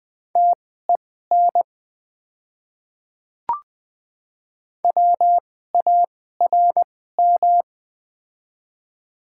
Morse code, heard as a keyed tone, about 20 wpm, spaced as for 14 wpm, 690 Hz. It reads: TEN WARM
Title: 2x Character Spacing - Top 500 Words - 20wpm